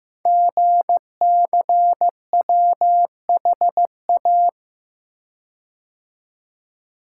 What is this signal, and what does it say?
Morse code, heard as a keyed tone, about 15 wpm, 700 Hz. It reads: GCWHA
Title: Code Groups - 5 Letter:Numbers - 15wpm